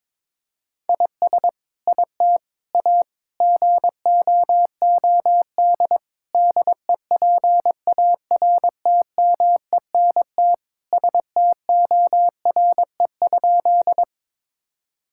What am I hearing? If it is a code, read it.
Morse code, heard as a keyed tone, about 21 wpm, 700 Hz. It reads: IS IT A GOOD DEPARTMENT STORE?
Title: Sentences from Top 1000 Words - Review Part 5 - 22wpm